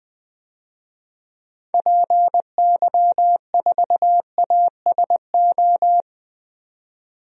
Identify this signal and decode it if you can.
Morse code, heard as a keyed tone, about 20 wpm, 700 Hz. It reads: PY4ASO
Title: DXCC-100 Call Signs - 6 Characters - 20wpm